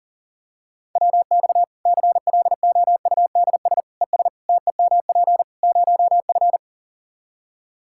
Morse code, heard as a keyed tone, about 40 wpm, 700 Hz. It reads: WX CLOUDS ES TEMP 0F
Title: Practice QSO - Exchange 2 - Info - 40wpm